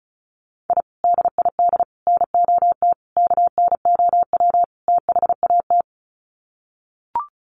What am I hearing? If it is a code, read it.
Morse code, heard as a keyed tone, about 34 wpm, 700 Hz. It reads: I DID NOT KNOW THAT
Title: Sets of 5 Words 35wpm